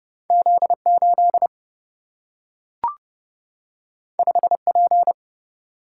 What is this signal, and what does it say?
Morse code, heard as a keyed tone, about 30 wpm, 700 Hz. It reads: Z 8 5 P
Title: Single-Letter Single-Number 30wpm